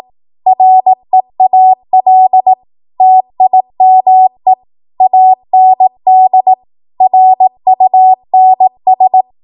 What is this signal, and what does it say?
Morse code, heard as a keyed tone, about 18 wpm, 750 Hz. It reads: REAL TIME AND RUNS
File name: Archive / 18 WPM / 18WPM